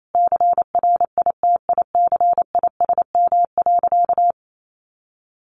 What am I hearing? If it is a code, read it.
Morse code, heard as a keyed tone, about 28 wpm, 700 Hz. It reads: CRITICISM.
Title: Sentences from Top 1300 Words 28wpm